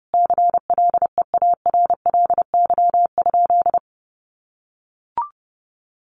Morse code, heard as a keyed tone, about 30 wpm, 700 Hz. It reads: CLEARLY?
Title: Sentences from Top 1300 Words 30wpm